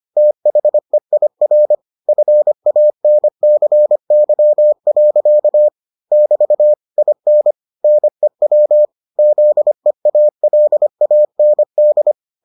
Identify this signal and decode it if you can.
Morse code, heard as a keyed tone, about 25 wpm, 600 Hz. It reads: THEIR FANCY. <BT> IN NEW ZEALAND